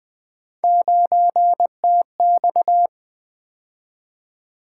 Morse code, heard as a keyed tone, about 20 wpm, 700 Hz. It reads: 9TX
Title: Three Letter-Number Combinations - 20wpm